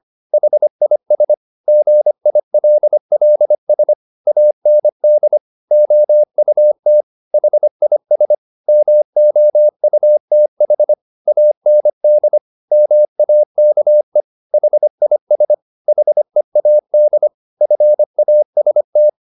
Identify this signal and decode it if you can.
Morse code, heard as a keyed tone, about 25 wpm, 600 Hz. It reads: HIS GILLS AND OUT HIS MOUTH AND MAKE HIS HEAD FAST